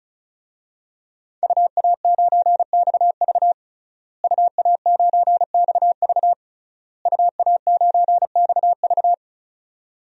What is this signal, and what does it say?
Morse code, heard as a keyed tone, about 36 wpm, 700 Hz. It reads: UA9XV UA9XV UA9XV
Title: DXCC-100 Call Signs - 5 Characters - Repeated 3x - 35wpm